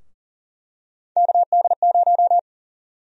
Morse code, heard as a keyed tone, about 40 wpm, 700 Hz. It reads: KD0
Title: USA Call Sign Prefixes - 40wpm